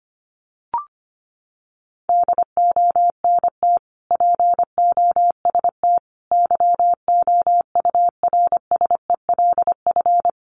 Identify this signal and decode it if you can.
Morse code, heard as a keyed tone, about 25 wpm, 700 Hz. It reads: DONT POST YOURSELF
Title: Sentences from Top 1000 Words 25wpm